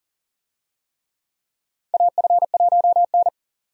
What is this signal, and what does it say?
Morse code, heard as a keyed tone, about 38 wpm, 700 Hz. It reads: AF1N
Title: USA Call Signs - 4 Characters - 40wpm